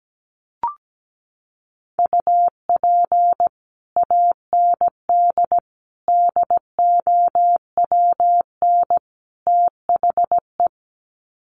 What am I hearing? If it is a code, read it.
Morse code, heard as a keyed tone, about 17 wpm, 700 Hz. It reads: UP AND DOWN THE
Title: Sets of 4 Words 17wpm